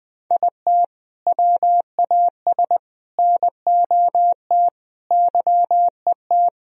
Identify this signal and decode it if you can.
Morse code, heard as a keyed tone, about 21 wpm, 700 Hz. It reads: IT WAS NOT YET
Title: Sentences from Top 1000 Words - Review Part 1 - 20wpm